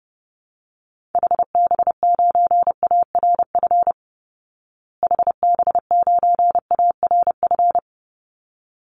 Morse code, heard as a keyed tone, about 30 wpm, 700 Hz. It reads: HB9ARF HB9ARF
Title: Call Signs Repeated 3x - 30wpm